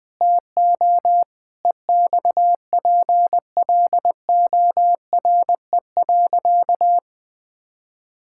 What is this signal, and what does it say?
Morse code, heard as a keyed tone, about 21 wpm, 700 Hz. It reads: TO EXPLORE.
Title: Sentences with 6-Word Phrases - 20wpm